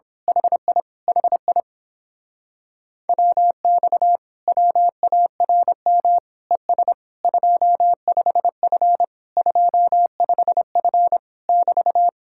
Morse code, heard as a keyed tone, about 26 wpm, 700 Hz. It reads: HI HI WX WARM ES 25F 25F <BT>
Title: Speed-Racing Common QSO Phrases - 22wpm